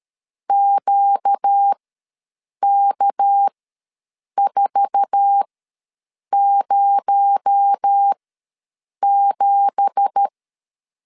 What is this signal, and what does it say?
Morse code, heard as a keyed tone, about 13 wpm, spaced as for 7 wpm, 800 Hz. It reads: QK407
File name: LEZIONI / 9 2867 ws 12wpm 3431